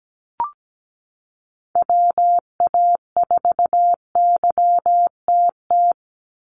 Morse code, heard as a keyed tone, about 17 wpm, 700 Hz. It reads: WA4YTT
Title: USA Call Signs 17wpm